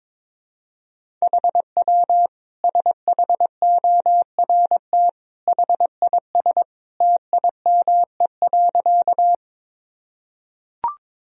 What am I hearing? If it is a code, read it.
Morse code, heard as a keyed tone, about 22 wpm, 700 Hz. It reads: HW SHORT HIS TIME.
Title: Sentences from Top 400 Words and Abbreviations 22wpm